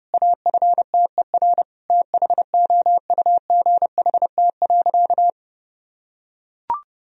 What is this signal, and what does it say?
Morse code, heard as a keyed tone, about 32 wpm, 700 Hz. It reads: AFTER THOUGHT.
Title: Sentences from Top 1000 Words - Review Part 2 - 30wpm